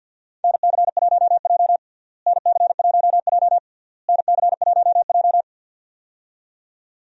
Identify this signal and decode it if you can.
Morse code, heard as a keyed tone, about 50 wpm, 700 Hz. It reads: NC1J NC1J NC1J
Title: USA Call Signs - 4 Characters - Repeated 3x - 50wpm